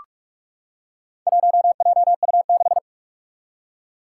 Morse code, heard as a keyed tone, about 41 wpm, 700 Hz. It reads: 1JU6
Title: Four Challenging Letter-Number Combinations 45wpm